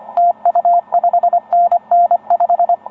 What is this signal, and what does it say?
Morse code, heard as a keyed tone, about 30 wpm, 690 Hz, with radio noise. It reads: TU 5NN 5